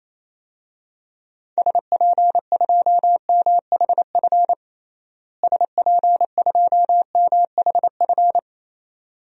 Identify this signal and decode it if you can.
Morse code, heard as a keyed tone, about 28 wpm, 700 Hz. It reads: SP2MHF SP2MHF
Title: DXCC-100 Call Signs - 6 Characters - Repeated 2x - 28wpm